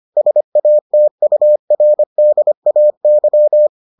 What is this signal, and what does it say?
Morse code, heard as a keyed tone, about 25 wpm, 600 Hz. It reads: SATURDAY